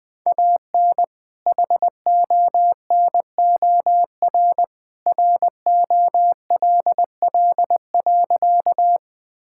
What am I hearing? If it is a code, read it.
Morse code, heard as a keyed tone, about 20 wpm, 700 Hz. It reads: AN HONOR ROLL.
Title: All-Age Appropriate Jokes - 20wpm